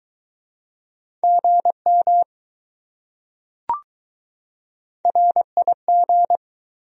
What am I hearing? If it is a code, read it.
Morse code, heard as a keyed tone, about 20 wpm, 700 Hz. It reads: GM RIG